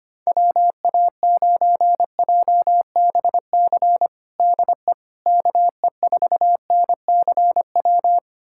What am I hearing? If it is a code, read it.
Morse code, heard as a keyed tone, about 25 wpm, 700 Hz. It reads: WA9JBC DE KE4NCW
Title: USA Callsign DE Callsign - 6 Characters - 25wpm